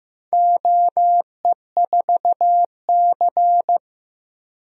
Morse code, heard as a keyed tone, about 15 wpm, 700 Hz. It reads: OE4C
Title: DXCC-100 Call Signs - 4 Characters - 15wpm